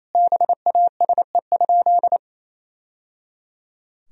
Morse code, heard as a keyed tone, about 28 wpm, 700 Hz. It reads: BASE?
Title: Sentences from Top 1000 Words - Review Part 1 - 28wpm